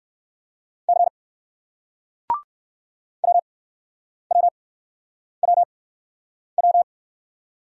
Morse code, heard as a keyed tone, about 61 wpm, spaced as for 9 wpm, 700 Hz. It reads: C WWWW